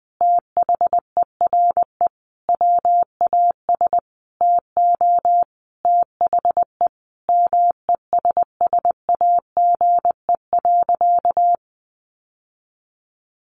Morse code, heard as a keyed tone, about 20 wpm, 700 Hz. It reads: THERE WAS TO THE MESSAGE.
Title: Sentences from Top 1100 Words 20wpm